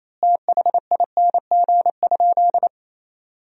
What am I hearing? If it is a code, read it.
Morse code, heard as a keyed tone, about 28 wpm, 700 Hz. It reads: THING?
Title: 4x Word-Spacing - Sentences from Top 200 Words - 28wpm